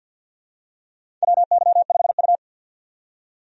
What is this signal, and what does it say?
Morse code, heard as a keyed tone, about 51 wpm, 700 Hz. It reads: WY5U
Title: USA Call Signs - 4 Characters - 50wpm